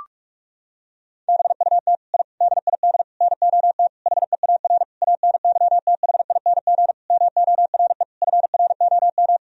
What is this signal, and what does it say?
Morse code, heard as a keyed tone, about 46 wpm, 700 Hz. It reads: BUT I DID NOT HEAR ANYTHING MORE FROM
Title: Sentences from Top 1000 Words - Review Part 4 - 45wpm